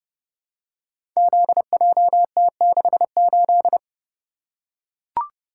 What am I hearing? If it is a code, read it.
Morse code, heard as a keyed tone, about 32 wpm, 700 Hz. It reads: ZJT68
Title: Code Groups - 5 Letter:Numbers - 30wpm